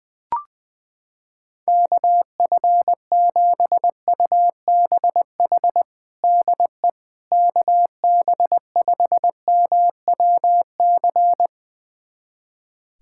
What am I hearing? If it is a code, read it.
Morse code, heard as a keyed tone, about 20 wpm, 700 Hz. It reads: KF7UBH DE KB5MWC
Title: USA Callsign DE Callsign - 6 Characters - 20wpm